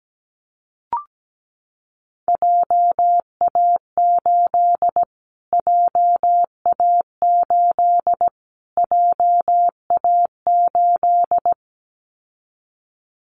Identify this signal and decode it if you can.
Morse code, heard as a keyed tone, about 17 wpm, 700 Hz. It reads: JA8 JA8 JA8
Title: Callsign Prefixes for Top 25 DXCC - Repeated 3x - 17wpm